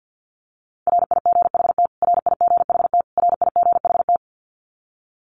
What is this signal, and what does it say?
Morse code, heard as a keyed tone, about 50 wpm, 700 Hz. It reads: RIGHT RIGHT RIGHT
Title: Top 500 Words Repeated 50wpm